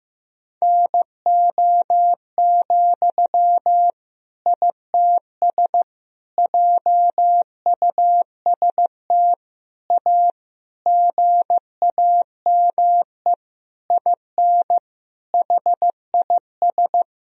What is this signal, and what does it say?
Morse code, heard as a keyed tone, about 15 wpm, 700 Hz. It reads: NO, ITS JUST A GAME IN HIS
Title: Sentences from Top 1000 Words - Review Part 3 - 15wpm